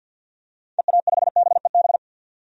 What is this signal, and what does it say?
Morse code, heard as a keyed tone, about 49 wpm, 700 Hz. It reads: EA5BEB
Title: DXCC-100 Call Signs - 6 Characters - 50wpm